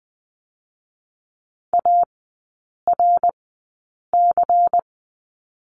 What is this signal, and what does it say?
Morse code, heard as a keyed tone, about 20 wpm, spaced as for 8 wpm, 700 Hz. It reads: ARC
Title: 198 Lesson 32e - Letter k - Three Characters 20wpm